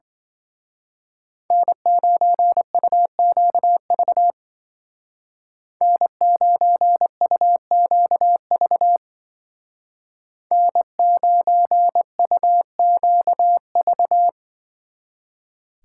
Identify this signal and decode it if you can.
Morse code, heard as a keyed tone, about 24 wpm, 700 Hz. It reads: N9UQV N9UQV N9UQV
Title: Speed Racing - USA 1x3 Call Signs 20wpm